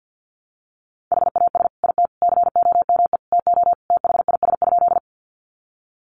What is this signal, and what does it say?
Morse code, heard as a keyed tone, about 51 wpm, 700 Hz. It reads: HAS IT COME TO THIS?
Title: Sentences from Top 100 Words 50wpm